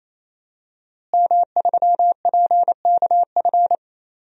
Morse code, heard as a keyed tone, about 27 wpm, 700 Hz. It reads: M3PKF
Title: DXCC-100 Call Signs - 5 Characters - Repeated 3x - 28wpm